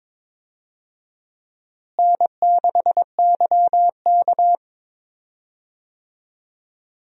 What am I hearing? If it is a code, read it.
Morse code, heard as a keyed tone, about 22 wpm, 700 Hz. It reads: N6YK
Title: USA Call Signs - 4 Characters - Repeated 3x - 22wpm